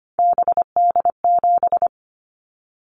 Morse code, heard as a keyed tone, about 25 wpm, 700 Hz. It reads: BD7
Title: Callsign Prefixes for Top 100 DXCC - 25wpm